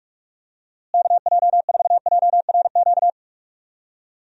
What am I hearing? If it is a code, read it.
Morse code, heard as a keyed tone, about 46 wpm, 700 Hz. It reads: KJ4JRQ
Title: USA Call Signs - 6 Characters - Repeated 2x - 45wpm